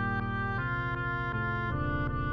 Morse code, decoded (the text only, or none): none